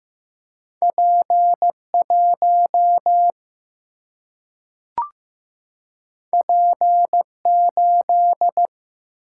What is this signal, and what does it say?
Morse code, heard as a keyed tone, about 15 wpm, 700 Hz. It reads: P 1 P 8
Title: Single-Letter Single-Number 15wpm